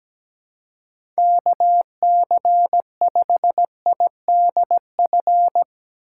Morse code, heard as a keyed tone, about 17 wpm, 700 Hz. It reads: KC5IDF
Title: USA Call Signs - 6 Characters - 2x Character Spacing - 17wpm